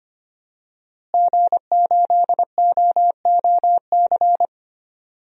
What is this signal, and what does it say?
Morse code, heard as a keyed tone, about 25 wpm, 700 Hz. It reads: G8OOC
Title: DXCC-100 Call Signs - 5 Characters - Repeated 3x - 25wpm